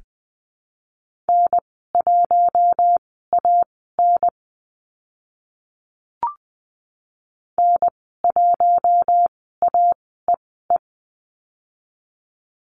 Morse code, heard as a keyed tone, about 20 wpm, spaced as for 14 wpm, 700 Hz. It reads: N1AN N1AEE
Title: 042 Lesson 8g - Number 1 - Call Signs 20wpm